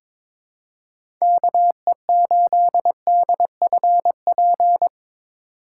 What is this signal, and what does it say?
Morse code, heard as a keyed tone, about 22 wpm, 700 Hz. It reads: KE8DFP
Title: USA Call Signs - 4x Character Spacing - 22wpm